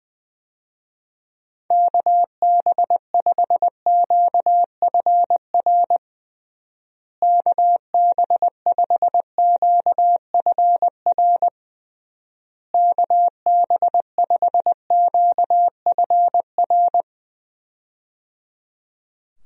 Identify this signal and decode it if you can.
Morse code, heard as a keyed tone, about 20 wpm, 700 Hz. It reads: KB5QFR KB5QFR KB5QFR
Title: USA Call Signs - 6 Characters - Repeated 3x - 20wpm